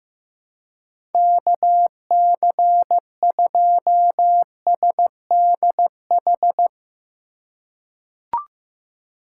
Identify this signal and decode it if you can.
Morse code, heard as a keyed tone, about 15 wpm, 700 Hz. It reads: KC2SDH
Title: USA Call Signs - 6 Characters - 4x Character Spacing - 15wpm